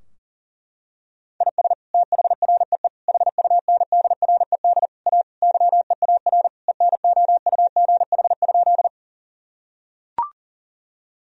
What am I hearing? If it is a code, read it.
Morse code, heard as a keyed tone, about 40 wpm, 700 Hz. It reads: IS THREE HUNDRED A YEAR ENOUGH?
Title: Sentences from Top 1000 Words - Review Part 2 - 40wpm